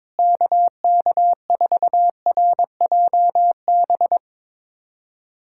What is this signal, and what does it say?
Morse code, heard as a keyed tone, about 22 wpm, 700 Hz. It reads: KK4RJB
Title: USA Callsign DE Callsign - 6 Characters - 22wpm